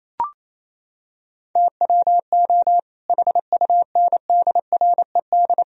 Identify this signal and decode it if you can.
Morse code, heard as a keyed tone, about 28 wpm, 700 Hz. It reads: TWO HUNDRED
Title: Sentences from Top 1000 Words - Review Part 2 - 28wpm